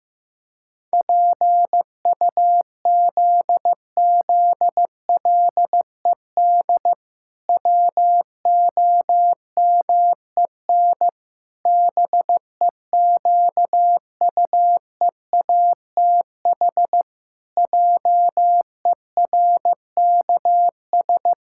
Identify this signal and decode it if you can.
Morse code, heard as a keyed tone, about 15 wpm, 700 Hz. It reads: PUZZLED WOMEN BEQUEATH JERKS
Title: Pangram Challenge v2 - 15wpm